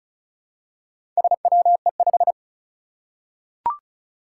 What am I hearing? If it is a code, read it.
Morse code, heard as a keyed tone, about 36 wpm, 700 Hz. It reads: SWE5
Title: Four Challenging Letter-Number Combinations 35wpm